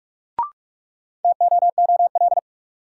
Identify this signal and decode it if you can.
Morse code, heard as a keyed tone, about 46 wpm, 700 Hz. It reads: TOOL
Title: Sentences from Top 1000 Words - Review Part 6 - New Word Drill - 45wpm